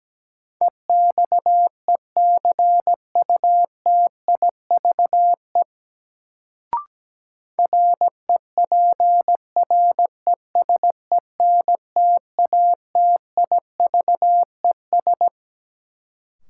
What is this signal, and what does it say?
Morse code, heard as a keyed tone, about 17 wpm, 700 Hz. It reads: EXECUTIVE REPRESENTATIVES
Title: Sets of 5 Words Encore - New Word Drill - 17wpm